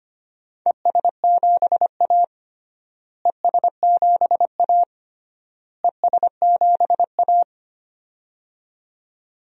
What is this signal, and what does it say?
Morse code, heard as a keyed tone, about 25 wpm, 700 Hz. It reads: ES7A ES7A ES7A
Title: DXCC-100 Call Signs - 4 Characters - Repeated 3x - 25wpm